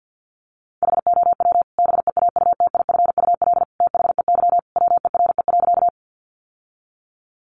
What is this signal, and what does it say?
Morse code, heard as a keyed tone, about 50 wpm, 700 Hz. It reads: HOW BEAUTIFUL THEY WERE.
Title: Sentences from Top 900 Words 50wpm